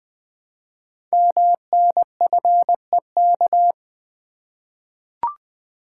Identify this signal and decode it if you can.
Morse code, heard as a keyed tone, about 20 wpm, 700 Hz. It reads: MNFEK
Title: Code Groups - 5 Letters - 20wpm